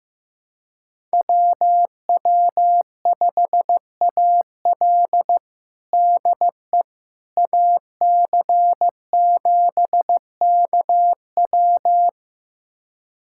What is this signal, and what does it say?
Morse code, heard as a keyed tone, about 15 wpm, 700 Hz. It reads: WW5AL DE AC7KW
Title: USA Callsign DE Callsign - 5 Characters - 15wpm